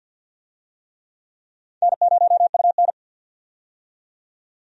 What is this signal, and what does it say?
Morse code, heard as a keyed tone, about 49 wpm, 700 Hz. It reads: N0UN
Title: USA Call Signs - 4 Characters - 50wpm